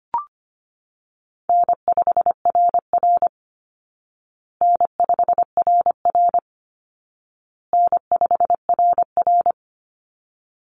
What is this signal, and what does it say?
Morse code, heard as a keyed tone, about 25 wpm, 700 Hz. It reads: N5RR N5RR N5RR
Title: Call Signs Repeated 2x - 25wpm